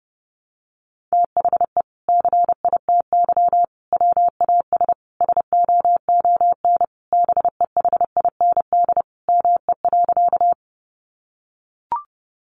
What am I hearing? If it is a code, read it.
Morse code, heard as a keyed tone, about 30 wpm, 700 Hz. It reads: THE CITY WAS SOON BEHIND ME.
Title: Sentences from Top 500 Words 30wpm